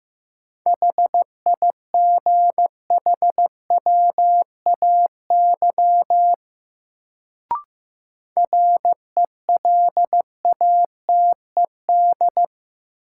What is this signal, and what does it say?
Morse code, heard as a keyed tone, about 15 wpm, 700 Hz. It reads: HIGHWAY RELATED
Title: Sentences from Top 1400 Words - New Word Drill Alt - 15wpm